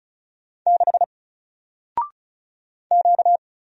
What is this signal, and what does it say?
Morse code, heard as a keyed tone, about 36 wpm, 700 Hz. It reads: 6 Q
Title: Single Letter-Number Alt 35wpm